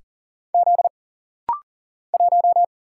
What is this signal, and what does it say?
Morse code, heard as a keyed tone, about 40 wpm, 700 Hz. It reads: Z 1
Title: Single Letter-Number - Rapid-Fire 40wpm